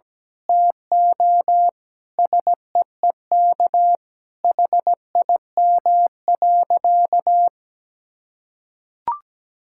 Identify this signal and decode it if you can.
Morse code, heard as a keyed tone, about 17 wpm, 700 Hz. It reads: TO SEEK HIM.
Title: Sentences from Top 1600 Words 17wpm